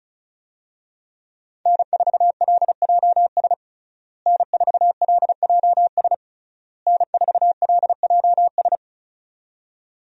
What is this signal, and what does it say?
Morse code, heard as a keyed tone, about 34 wpm, 700 Hz. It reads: N4LJS N4LJS N4LJS
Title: USA Call Signs - 5 Characters - Repeated 3x - 35wpm